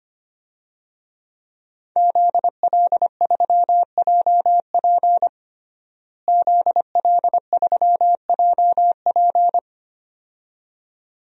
Morse code, heard as a keyed tone, about 25 wpm, 700 Hz. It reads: ZL3JP ZL3JP
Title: DXCC-100 Call Signs - 5 Characters - Repeated 2x - 25wpm